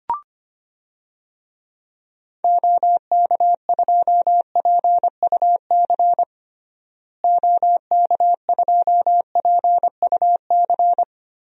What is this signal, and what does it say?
Morse code, heard as a keyed tone, about 25 wpm, 700 Hz. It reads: OK2PUC OK2PUC
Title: DXCC-100 Call Signs - 6 Characters - Repeated 3x - 25wpm